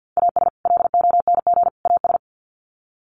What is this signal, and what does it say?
Morse code, heard as a keyed tone, about 50 wpm, 700 Hz. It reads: AS LONG AS
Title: Sets of 3 Words 50wpm